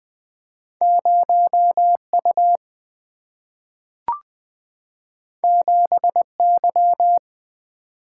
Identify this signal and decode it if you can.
Morse code, heard as a keyed tone, about 20 wpm, 700 Hz. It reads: 0U 7Y
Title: Two Letter-Number Combinations - 20wpm